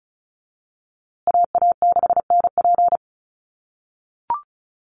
Morse code, heard as a keyed tone, about 34 wpm, 700 Hz. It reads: AA6NP
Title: USA Call Signs Repeated 3x 35wpm